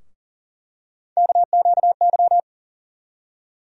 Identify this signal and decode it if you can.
Morse code, heard as a keyed tone, about 40 wpm, 700 Hz. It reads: KQY
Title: USA Call Sign Postfixes - 3 Letters 40wpm